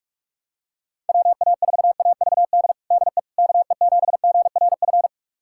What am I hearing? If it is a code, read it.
Morse code, heard as a keyed tone, about 43 wpm, 700 Hz. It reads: WA4AVD DE KE7GRF
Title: USA Callsign DE Callsign 45wpm